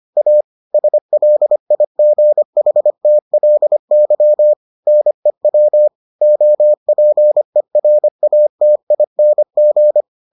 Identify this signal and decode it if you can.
Morse code, heard as a keyed tone, about 25 wpm, 600 Hz. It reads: A SLIGHTLY NEW OPERATING